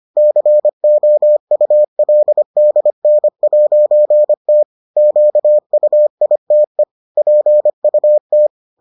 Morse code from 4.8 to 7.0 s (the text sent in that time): QUITE